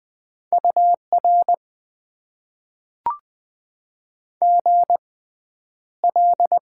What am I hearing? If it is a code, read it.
Morse code, heard as a keyed tone, about 20 wpm, spaced as for 10 wpm, 700 Hz. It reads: UR GL
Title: Top 90 QSO Elements - 6x Character Spacing - 20wpm